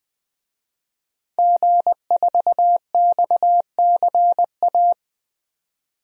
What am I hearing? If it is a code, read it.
Morse code, heard as a keyed tone, about 20 wpm, 700 Hz. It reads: G4XCA
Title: DXCC-100 Call Signs - 5 Characters - 20wpm